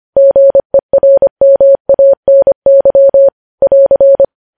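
Morse code, heard as a keyed tone, about 25 wpm, 550 Hz. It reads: GERMANY <AR>
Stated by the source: News headlines sent in Morse at 25.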